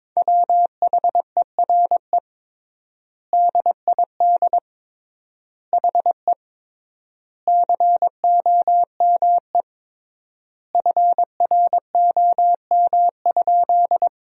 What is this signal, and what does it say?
Morse code, heard as a keyed tone, about 22 wpm, 700 Hz. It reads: WHERE DID HE COME FROM?
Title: 3x Word-Spacing - Sentences from Top 200 Words - 22wpm